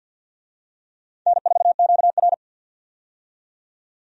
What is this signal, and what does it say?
Morse code, heard as a keyed tone, about 46 wpm, 700 Hz. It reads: N4QR
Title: USA Call Signs - 4 Characters - 50wpm